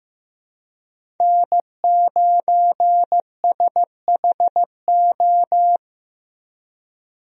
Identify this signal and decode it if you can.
Morse code, heard as a keyed tone, about 15 wpm, 700 Hz. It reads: N9SHO
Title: USA Call Signs - 5 Characters - Repeated 2x - 15wpm